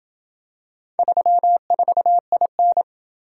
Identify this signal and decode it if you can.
Morse code, heard as a keyed tone, about 27 wpm, 700 Hz. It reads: 34IN